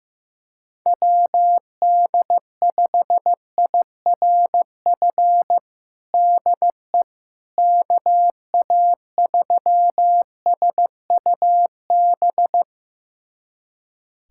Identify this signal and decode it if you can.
Morse code, heard as a keyed tone, about 15 wpm, 700 Hz. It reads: WD5IRF DE KA3SUB
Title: USA Callsign DE Callsign - 6 Characters - 15wpm